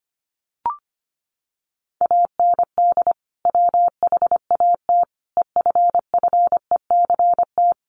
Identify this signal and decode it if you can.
Morse code, heard as a keyed tone, about 25 wpm, 700 Hz. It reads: AND WHAT EFFECT
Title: Sentences from Top 1000 Words 25wpm